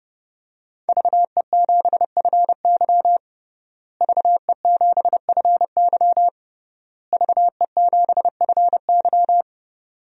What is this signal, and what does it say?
Morse code, heard as a keyed tone, about 32 wpm, 700 Hz. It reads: VE7FY VE7FY VE7FY
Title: DXCC-100 Call Signs - 5 Characters - Repeated 3x - 30wpm